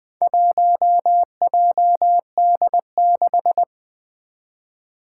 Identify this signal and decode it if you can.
Morse code, heard as a keyed tone, about 20 wpm, 700 Hz. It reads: 1JD6
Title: Four Challenging Letter-Number Combinations 20wpm